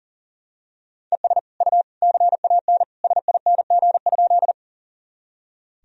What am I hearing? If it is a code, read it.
Morse code, heard as a keyed tone, about 40 wpm, 700 Hz. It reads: ES U CAN SING?
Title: Sentences from Top 400 Words and Abbreviations 40wpm